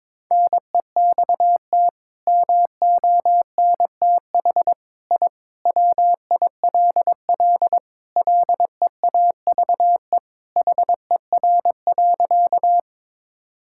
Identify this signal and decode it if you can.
Morse code, heard as a keyed tone, about 21 wpm, 700 Hz. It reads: NEXT MONTH I WILL LEAVE HER.
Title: Sentences from Top 1000 Words - Review Part 5 - 22wpm